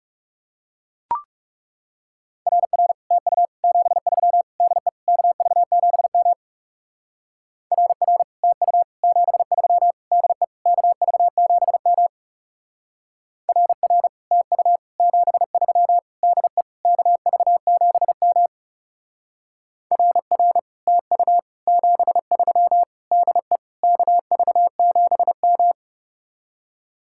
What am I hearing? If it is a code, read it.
Morse code, heard as a keyed tone, about 36 wpm, 700 Hz. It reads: RR TU 73 DE KV7M RR TU 73 DE KV7M RR TU 73 DE KV7M RR TU 73 DE KV7M
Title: Speed-Racing Common QSO Phrases - 30wpm